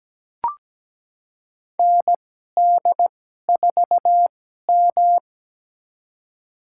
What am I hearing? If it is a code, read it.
Morse code, heard as a keyed tone, about 17 wpm, spaced as for 12 wpm, 700 Hz. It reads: ND4M
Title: USA Call Signs - 4 Characters - 2x Character Spacing - 17wpm